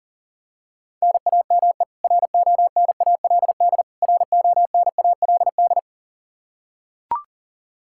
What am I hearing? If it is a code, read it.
Morse code, heard as a keyed tone, about 40 wpm, 700 Hz. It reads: NAME RONALD RONALD
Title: Practice QSO - Exchange 1 - Beginning - 40wpm